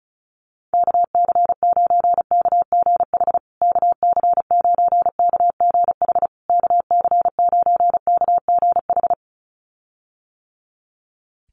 Morse code, heard as a keyed tone, about 34 wpm, 700 Hz. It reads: KC9KGH KC9KGH KC9KGH
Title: USA Call Signs Repeated 3x 35wpm